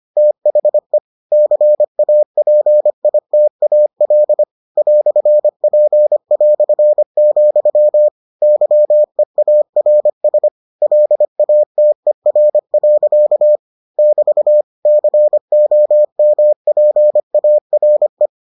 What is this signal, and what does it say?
Morse code, heard as a keyed tone, about 25 wpm, 600 Hz. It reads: THE CAPITAL "P", YEARS LATER. <BT> COMPARE